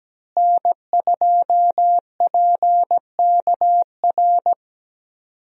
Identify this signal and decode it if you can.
Morse code, heard as a keyed tone, about 17 wpm, 700 Hz. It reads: N2PKR
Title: USA Call Signs - 5 Characters - Repeated 3x - 17wpm